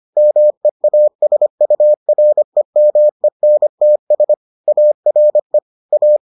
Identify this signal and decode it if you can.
Morse code, heard as a keyed tone, about 25 wpm, 600 Hz. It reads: MEASUREMENTS ARE A